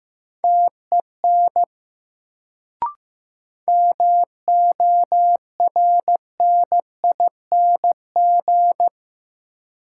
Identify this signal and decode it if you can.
Morse code, heard as a keyed tone, about 15 wpm, 700 Hz. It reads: TEN MORNING
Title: Sentences from Top 400 Words - New Word Drill - 15wpm